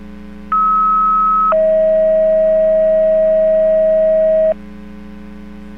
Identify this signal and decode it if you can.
Two-tone page, heard as a one-second tone, then a three-second tone, about 1290 then 640 Hz. It is heard from 0.5 to 4.5 s.